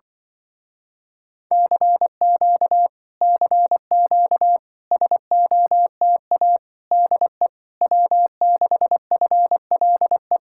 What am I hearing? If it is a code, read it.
Morse code, heard as a keyed tone, about 24 wpm, 700 Hz. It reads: CQ CQ SOTA DE W6FLE
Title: Speed-Racing Common QSO Phrases Encore - 20wpm